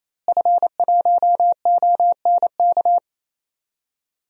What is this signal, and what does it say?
Morse code, heard as a keyed tone, about 28 wpm, 700 Hz. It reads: F1ONK
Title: DXCC-100 Call Signs - 5 Characters - Repeated 3x - 28wpm